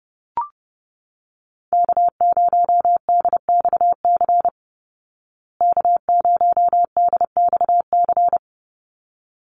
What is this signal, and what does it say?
Morse code, heard as a keyed tone, about 30 wpm, 700 Hz. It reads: K0DXC K0DXC
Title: Call Signs Repeated 3x - 30wpm